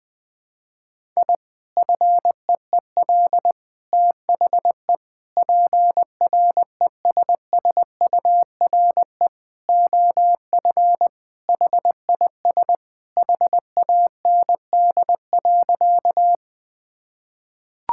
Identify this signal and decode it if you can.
Morse code, heard as a keyed tone, about 20 wpm, 700 Hz. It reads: I FEEL THE PRESSURE OF HIS HAND.
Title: Sentences from Top 1000 Words - Review Part 6 - 20wpm